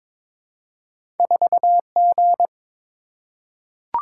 Morse code, heard as a keyed tone, about 22 wpm, 700 Hz. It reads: 4G
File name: practice / 22wpm / Two Letter-Number Combinations - 22wpm